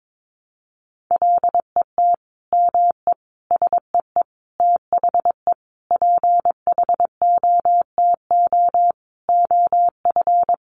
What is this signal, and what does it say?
Morse code, heard as a keyed tone, about 22 wpm, 700 Hz. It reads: LET ME SEE THE PHOTO OF
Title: Sentences from Top 1200 Words 22wpm